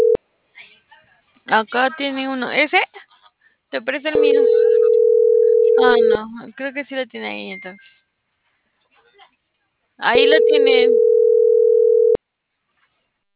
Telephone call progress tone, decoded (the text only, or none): ringback tone